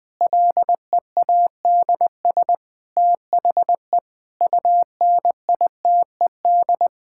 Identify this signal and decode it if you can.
Morse code, heard as a keyed tone, about 20 wpm, 700 Hz. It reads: LEADS THE UNITED
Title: Sentences with 5-Word Phrases - 20wpm